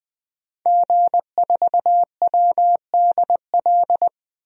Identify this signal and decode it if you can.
Morse code, heard as a keyed tone, about 21 wpm, 700 Hz. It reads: G4WDL
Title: DXCC-100 Call Signs - 5 Characters - 20wpm